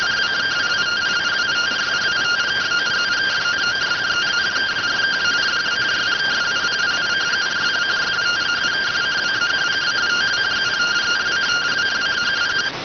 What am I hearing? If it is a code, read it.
Two-tone teleprinter signal, about 170 Hz shift.